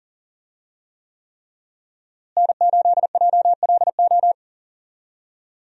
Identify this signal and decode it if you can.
Morse code, heard as a keyed tone, about 40 wpm, 700 Hz. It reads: N8JLO
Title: USA Call Signs - 5 Characters - Repeated 3x - 40wpm